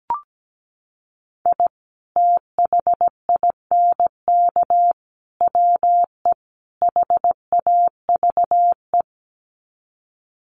Morse code, heard as a keyed tone, about 17 wpm, 700 Hz. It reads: I THINK WE HAVE
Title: Sets of 4 Words 17wpm